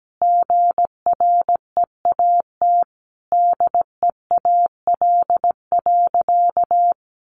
Morse code, heard as a keyed tone, about 17 wpm, 700 Hz. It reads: GREAT DEAL.
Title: Sentences from Top 1300 Words 17wpm